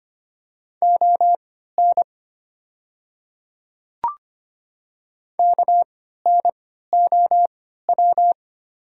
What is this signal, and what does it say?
Morse code, heard as a keyed tone, about 25 wpm, spaced as for 14 wpm, 690 Hz. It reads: ON KNOW
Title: 3x Character Spacing - Top 100 Words - 25wpm